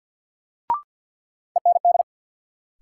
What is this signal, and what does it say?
Morse code, heard as a keyed tone, about 53 wpm, 700 Hz. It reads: END
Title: Sentences from Top 200 Words - New Word Drill - 50wpm